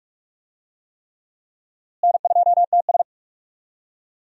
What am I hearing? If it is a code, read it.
Morse code, heard as a keyed tone, about 44 wpm, 700 Hz. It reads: N2TS
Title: USA Call Signs - 4 Characters - Repeated 3x - 45wpm